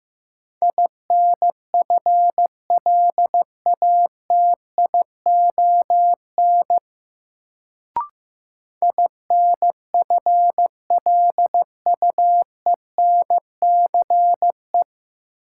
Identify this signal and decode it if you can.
Morse code, heard as a keyed tone, about 15 wpm, 700 Hz. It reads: INFLATION INFLUENCE
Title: Sentences from Top 1100 Words - New Word Drill - 15wpm